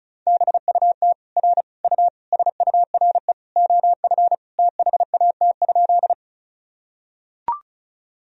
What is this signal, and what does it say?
Morse code, heard as a keyed tone, about 36 wpm, 700 Hz. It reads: BUT R U SURE OF THAT?
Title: Sentences from Top 300 Words and Abbreviations 35wpm